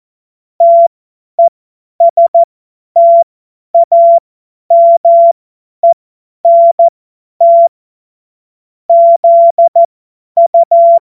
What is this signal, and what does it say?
Morse code, heard as a keyed tone, about 14 wpm, spaced as for 10 wpm, 680 Hz. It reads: TESTAMENT ZU
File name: Morsekurs / Witze / Witz 17. Mai 2022 50 F70